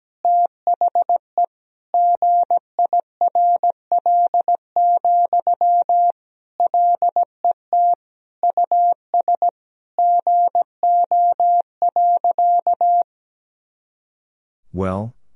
Morse code, heard as a keyed tone, about 17 wpm, 700 Hz. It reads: THE GIRL, LET US GO.
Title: Sentences from Top 400 Words and Abbreviations 17wpm